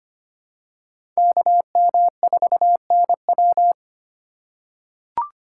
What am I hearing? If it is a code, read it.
Morse code, heard as a keyed tone, about 25 wpm, 700 Hz. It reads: KM4NW
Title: USA Call Signs - 5 Characters - 4x Character Spacing - 25wpm